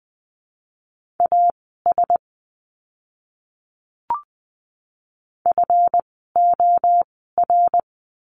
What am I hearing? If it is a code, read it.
Morse code, heard as a keyed tone, about 20 wpm, spaced as for 15 wpm, 700 Hz. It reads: AS FOR
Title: 237 Lesson 38f - Letter q - Words 20wpm